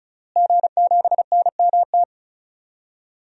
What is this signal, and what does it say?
Morse code, heard as a keyed tone, about 36 wpm, 700 Hz. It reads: G7NMT
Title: DXCC-100 Call Signs - 5 Characters - Repeated 3x - 35wpm